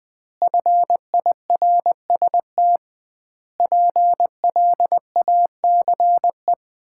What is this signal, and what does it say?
Morse code, heard as a keyed tone, about 20 wpm, 700 Hz. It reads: FIRST PLACE
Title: 2x Word-Spacing - Sets of 4 Words - 20wpm